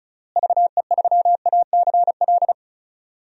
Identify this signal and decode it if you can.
Morse code, heard as a keyed tone, about 36 wpm, 700 Hz. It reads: VE3ACL
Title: DXCC-100 Call Signs - 6 Characters - Repeated 3x - 35wpm